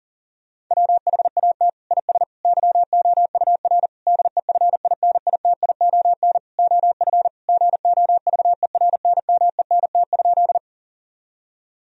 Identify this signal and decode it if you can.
Morse code, heard as a keyed tone, about 40 wpm, 700 Hz. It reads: WHAT IS YOUR DEFINITION OF GOVERNMENT?
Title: Sentences from Top 1000 Words - Review Part 4 - 40wpm